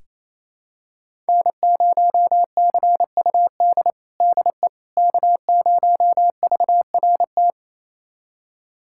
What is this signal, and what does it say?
Morse code, heard as a keyed tone, about 28 wpm, 700 Hz. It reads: N0CUD DE K0VRT
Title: USA Callsign DE Callsign - 5 Characters - 28wpm